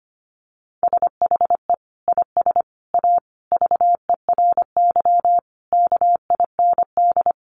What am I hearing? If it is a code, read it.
Morse code, heard as a keyed tone, about 25 wpm, 700 Hz. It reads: SHE IS A VERY KIND
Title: Sentences from Top 1100 Words 25wpm